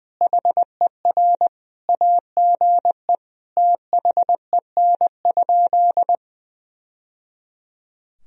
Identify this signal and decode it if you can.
Morse code, heard as a keyed tone, about 20 wpm, 700 Hz. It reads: HER AGE THEN?
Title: Sentences from Top 1000 Words - Review Part 3 - 20wpm